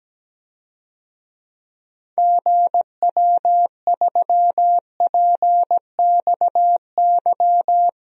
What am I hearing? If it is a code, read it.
Morse code, heard as a keyed tone, about 17 wpm, 700 Hz. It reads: GW3PXY
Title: DXCC-100 Call Signs - 6 Characters - Repeated 3x - 17wpm